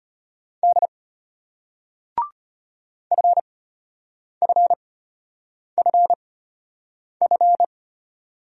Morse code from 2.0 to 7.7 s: FFFF

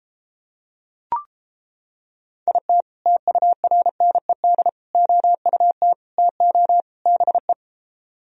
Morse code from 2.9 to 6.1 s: TURNED OUT

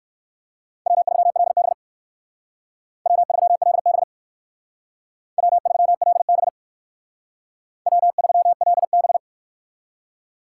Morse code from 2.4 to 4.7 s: W3LB

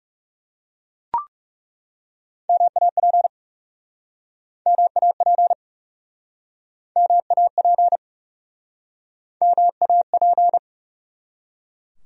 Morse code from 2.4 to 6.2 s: MAP MAP

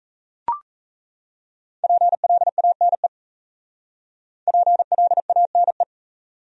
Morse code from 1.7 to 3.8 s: PLANE

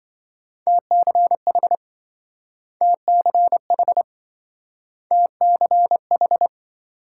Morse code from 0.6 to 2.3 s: TCH

